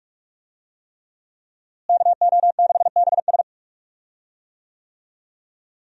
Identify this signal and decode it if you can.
Morse code, heard as a keyed tone, about 46 wpm, 700 Hz. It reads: KO6BS
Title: USA Call Signs - 5 Characters - Repeated 3x - 45wpm